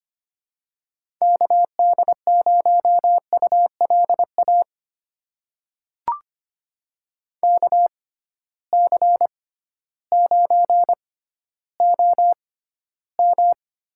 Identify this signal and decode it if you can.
Morse code, heard as a keyed tone, about 25 wpm, spaced as for 14 wpm, 700 Hz. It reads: KD0ULA KC9OM
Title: USA Call Signs - 6 Characters - 6x Character Spacing - 25wpm